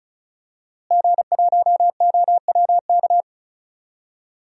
Morse code, heard as a keyed tone, about 36 wpm, 700 Hz. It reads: G1OWK
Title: DXCC-100 Call Signs - 5 Characters - Repeated 3x - 35wpm